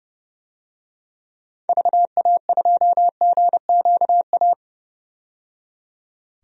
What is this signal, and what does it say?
Morse code, heard as a keyed tone, about 32 wpm, 700 Hz. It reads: VA2GQA
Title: DXCC-100 Call Signs - 6 Characters - 30wpm